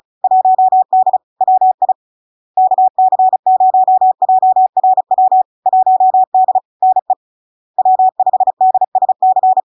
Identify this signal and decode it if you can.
Morse code, heard as a keyed tone, about 35 wpm, 750 Hz. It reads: 1D WI KC0JRW 1D NE W5DSC
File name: LARK-FD-2022-Log-in-CW-35wpm-750-Hz